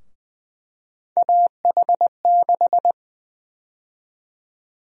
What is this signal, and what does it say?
Morse code, heard as a keyed tone, about 21 wpm, 700 Hz. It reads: AH6
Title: USA Call Sign Prefixes - Repeated 2x - 20wpm